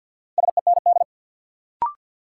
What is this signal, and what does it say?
Morse code, heard as a keyed tone, about 53 wpm, 700 Hz. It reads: SEND
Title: Sentences from Top 1500 Words - New Word Drill Alt - 50wpm